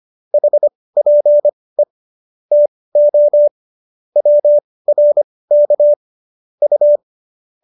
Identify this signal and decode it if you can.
Morse code, heard as a keyed tone, about 25 wpm, spaced as for 18 wpm, 590 Hz. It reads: HPE TO WRK U